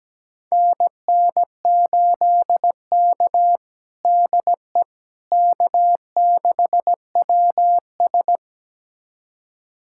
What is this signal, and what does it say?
Morse code, heard as a keyed tone, about 17 wpm, 700 Hz. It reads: NN8K DE K6WS
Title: USA Callsign DE Callsign - 4 Characters - 17wpm